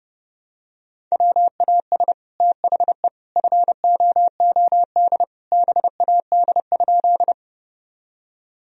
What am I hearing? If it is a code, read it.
Morse code, heard as a keyed tone, about 28 wpm, 700 Hz. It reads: WAS THE FOOD BAD?
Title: Sentences from Top 1000 Words - Review Part 1 - 30wpm